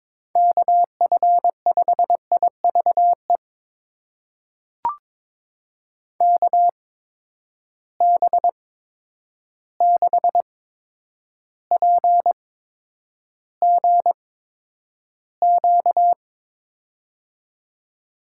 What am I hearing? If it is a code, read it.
Morse code, heard as a keyed tone, about 21 wpm, spaced as for 9 wpm, 700 Hz. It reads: KF5IVE KB6PGQ
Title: USA Call Signs - 6 Characters - 8x Character Spacing - 22wpm